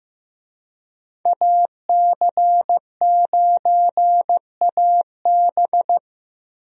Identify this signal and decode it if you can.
Morse code, heard as a keyed tone, about 15 wpm, 700 Hz. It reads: AC9AB
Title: USA Call Signs - 5 Characters - Repeated 2x - 15wpm